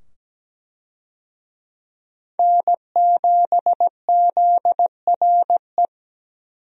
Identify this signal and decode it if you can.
Morse code, heard as a keyed tone, about 17 wpm, 700 Hz. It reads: N7ZRE
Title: USA Call Signs - 5 Characters - 17wpm